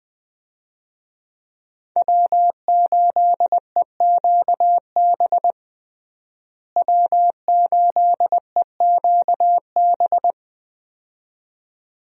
Morse code, heard as a keyed tone, about 20 wpm, 700 Hz. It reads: W8EQB W8EQB
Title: USA Call Signs - 5 Characters - Repeated 2x - 20wpm